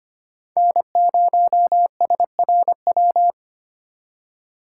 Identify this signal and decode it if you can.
Morse code, heard as a keyed tone, about 25 wpm, 700 Hz. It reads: N0SRW
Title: USA Call Signs - 5 Characters - 2x Character Spacing - 25wpm